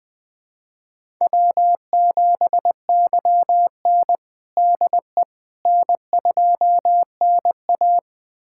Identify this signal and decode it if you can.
Morse code, heard as a keyed tone, about 21 wpm, 700 Hz. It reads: W7YN DE N2NA
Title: USA Callsign DE Callsign - 4 Characters - 20wpm